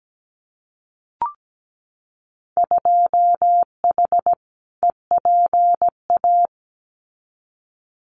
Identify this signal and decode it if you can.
Morse code, heard as a keyed tone, about 17 wpm, 700 Hz. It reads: 2H EPA
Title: Winter Field Day Exchanges 17wpm